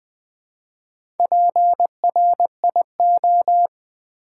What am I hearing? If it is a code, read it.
Morse code, heard as a keyed tone, about 20 wpm, 700 Hz. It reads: PRIO
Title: Build A Word - 2 Letters At A Time - 8-Letter Words - 20wpm